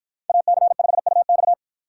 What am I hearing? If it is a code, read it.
Morse code, heard as a keyed tone, about 49 wpm, 700 Hz. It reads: AC5UX